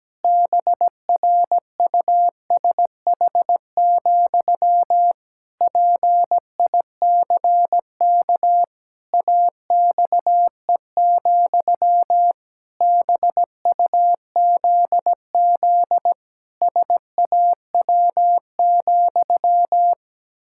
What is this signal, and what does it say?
Morse code, heard as a keyed tone, about 17 wpm, 700 Hz. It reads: BRUSH, PICK AXE, BUZZ SAW,
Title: Pangram Challenge v2 - 17wpm